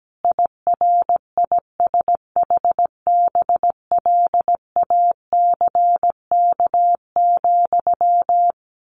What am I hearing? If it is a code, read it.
Morse code, heard as a keyed tone, about 17 wpm, 700 Hz. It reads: IRISHBLACK,
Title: A Sound Of Thunder 17wpm